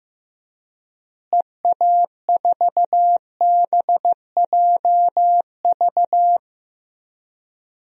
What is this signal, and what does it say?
Morse code, heard as a keyed tone, about 16 wpm, 700 Hz. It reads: EA4BJV
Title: DXCC-100 Call Signs - 6 Characters - Repeated 3x - 15wpm